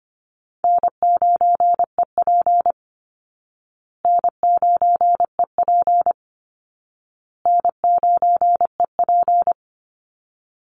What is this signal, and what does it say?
Morse code, heard as a keyed tone, about 25 wpm, 700 Hz. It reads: N9EP N9EP N9EP
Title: Easy Call Signs Repeated 3x - 25wpm